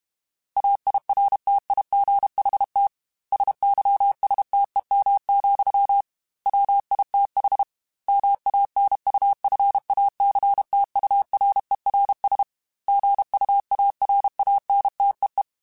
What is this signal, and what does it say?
Morse code, heard as a keyed tone, about 32 wpm, 800 Hz. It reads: AIRTIGHT SYSTEM, WITH MANUFACTURERS GUARANTEE
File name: Quotes / Quotes-mp3s / Airtight @30